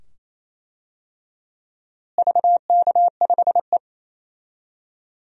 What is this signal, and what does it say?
Morse code, heard as a keyed tone, about 27 wpm, 700 Hz. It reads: VK5E
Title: DXCC-100 Call Signs - 4 Characters - Repeated 2x - 28wpm